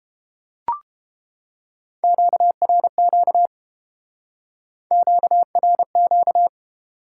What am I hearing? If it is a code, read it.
Morse code, heard as a keyed tone, about 29 wpm, 700 Hz. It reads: QRQ QRQ